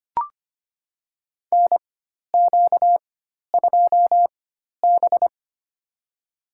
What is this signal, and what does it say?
Morse code, heard as a keyed tone, about 25 wpm, spaced as for 12 wpm, 700 Hz. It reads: NQ2B
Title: USA Call Signs - 4 Characters - 4x Character Spacing - 25wpm